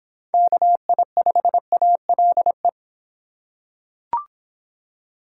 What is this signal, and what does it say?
Morse code, heard as a keyed tone, about 25 wpm, 700 Hz. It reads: KI5ALE